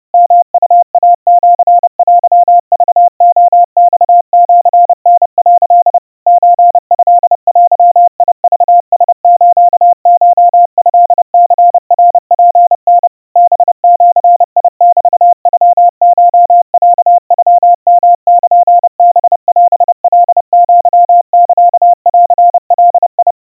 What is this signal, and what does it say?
Morse code (Japanese Wabun code), heard as a keyed tone, about 30 wpm, 700 Hz. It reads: ヨウイシテクレマシタ」 ソトデクラスコトニナツタ ハジメノコロノヨルハオカアサンガ
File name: kawa / Morse sample / kamyu 110 cpm 700 Hz Crispy